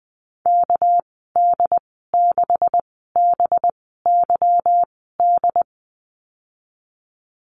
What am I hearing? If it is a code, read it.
Morse code, heard as a keyed tone, about 19 wpm, spaced as for 14 wpm, 700 Hz. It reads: KD6BYD
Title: 200 Lesson 32g - Letter k - Call Signs 20wpm